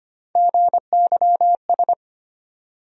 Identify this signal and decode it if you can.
Morse code, heard as a keyed tone, about 25 wpm, 700 Hz. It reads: GYS
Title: USA Call Sign Postfixes - 3 Letters 25wpm